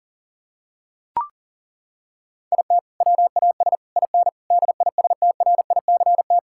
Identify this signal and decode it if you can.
Morse code, heard as a keyed tone, about 40 wpm, 700 Hz. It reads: IT WAS IN DISTRICT
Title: Sentences from Top 1600 Words 40wpm